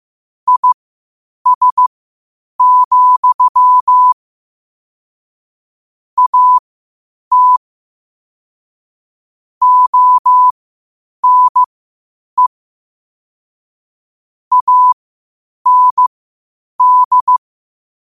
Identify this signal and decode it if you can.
Morse code, heard as a keyed tone, about 15 wpm, spaced as for 8 wpm, 1000 Hz. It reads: IS, AT ONE AND
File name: qotdfiles / 8wpm